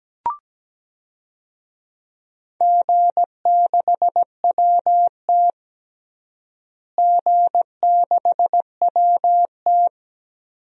Morse code, heard as a keyed tone, about 17 wpm, 700 Hz. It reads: G6WT G6WT
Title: DXCC-100 Call Signs - 4 Characters - Repeated 2x - 17wpm